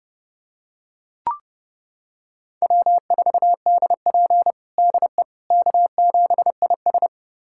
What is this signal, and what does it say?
Morse code, heard as a keyed tone, about 32 wpm, 700 Hz. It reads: W4DP DE K7IS
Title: USA Callsign DE Callsign - 4 Characters - 30wpm